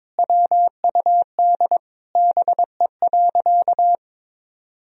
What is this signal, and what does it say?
Morse code, heard as a keyed tone, about 22 wpm, 700 Hz. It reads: WUD BE.
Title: Sentences from Top 200 Words and Abbreviations 22wpm